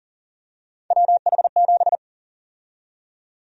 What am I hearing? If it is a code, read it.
Morse code, heard as a keyed tone, about 37 wpm, 700 Hz. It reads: WH7
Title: USA Call Sign Prefixes - 40wpm